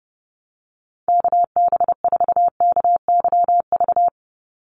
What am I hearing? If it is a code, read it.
Morse code, heard as a keyed tone, about 30 wpm, 700 Hz. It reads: KB4KYV
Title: USA Call Signs 30wpm